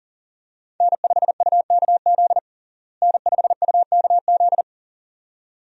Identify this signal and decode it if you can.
Morse code, heard as a keyed tone, about 40 wpm, 700 Hz. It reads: N5UKZ N5UKZ
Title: USA Call Signs - 5 Characters - Repeated 2x - 40wpm